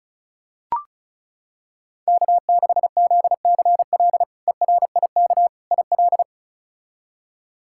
Morse code, heard as a keyed tone, about 36 wpm, 700 Hz. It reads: K6ZCL ERIK IL
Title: K1USN SST QSO Practice - 35wpm